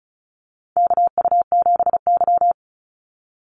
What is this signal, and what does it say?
Morse code, heard as a keyed tone, about 34 wpm, 700 Hz. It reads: KU7Y
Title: Call Signs Repeated 2x - 35wpm